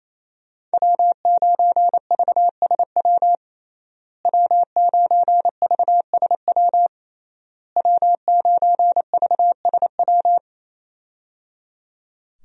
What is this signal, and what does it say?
Morse code, heard as a keyed tone, about 27 wpm, 700 Hz. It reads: W9VSW W9VSW W9VSW
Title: USA Call Signs - 5 Characters - Repeated 3x - 28wpm